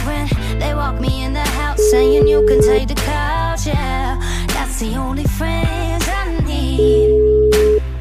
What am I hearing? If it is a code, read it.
Ringback tone (Europe, 425 Hz), with noise.